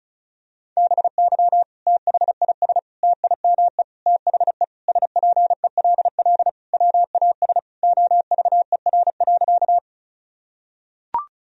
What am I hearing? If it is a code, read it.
Morse code, heard as a keyed tone, about 34 wpm, 700 Hz. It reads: BY THIS TIME THE SPELL WAS OVER.
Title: Sentences from Top 1000 Words - Review Part 1 - 35wpm